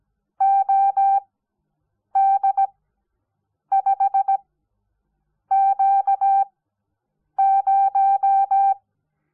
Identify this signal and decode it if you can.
Morse code, heard as a keyed tone, about 17 wpm, spaced as for 7 wpm, 780 Hz. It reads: OD5Q0
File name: Onlinelehrgang / morsen / DJ4UF Morsen05